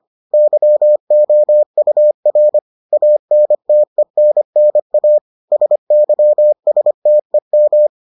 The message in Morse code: YOUR ANTENNA SYSTEM

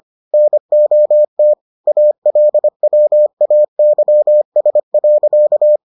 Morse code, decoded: NOT ALWAYS.

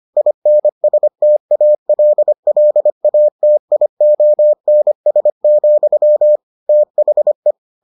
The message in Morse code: INSTALLATIONS, THE

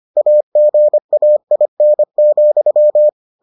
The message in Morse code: AGAIN,

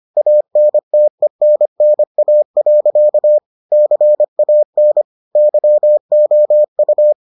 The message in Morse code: ANTENNA. CAN YOU

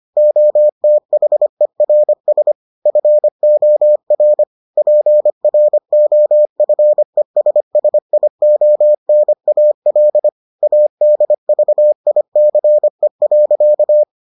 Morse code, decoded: OTHERS FOR PROFESSIONAL ADVICE.

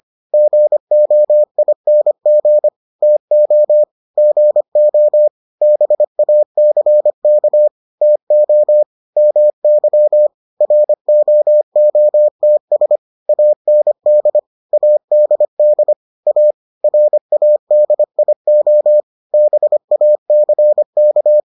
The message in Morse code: GOING TO GO BACK TO MY ROOTS AND ADD A RADIO BACK

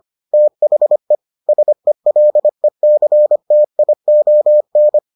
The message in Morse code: THE SELECTION

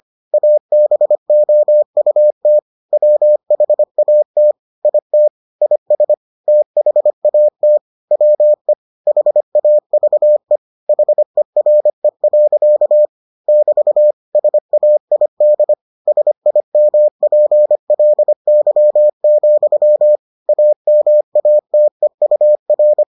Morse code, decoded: ABOUT WHAT IT IS THAT WE HAVE HERE. <BT> SAID SIMPLY, AMATEUR

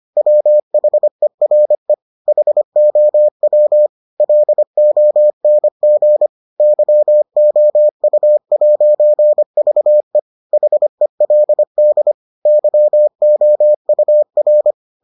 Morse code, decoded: WHERE HOW LONG YOU'VE HELD YOUR

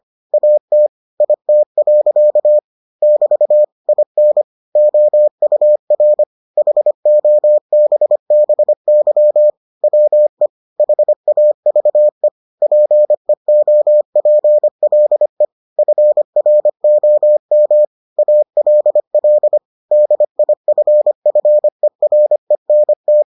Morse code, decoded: AT IT. <BT> IN OUR HOBBY WE HAVE PEOPLE FROM ALL DIFFERENT

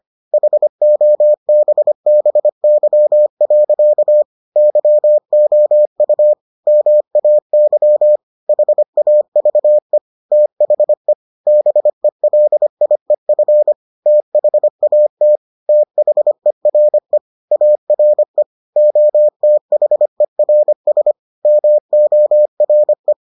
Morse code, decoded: HOBBY. YOU MAY HAVE THE BELIEF THAT THERE ARE OTHERS MORE

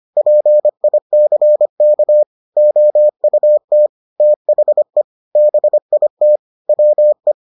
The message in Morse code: PICK OUT THE BIT WE

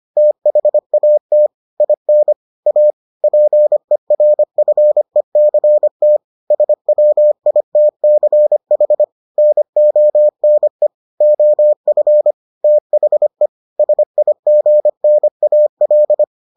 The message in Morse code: THAT IN A PERFECT SWITCH NONE OF THE SIGNAL

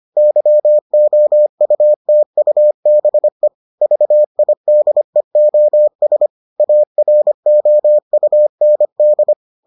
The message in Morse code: YOUTUBE VIDEOS AROUND

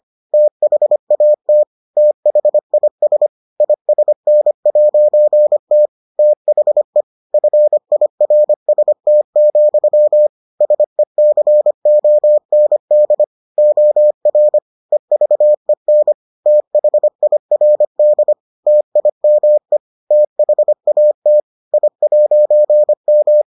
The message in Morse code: THAT THIS ISN'T THE FIRST, SECOND OR EVEN THIRD TIME THAT I'M

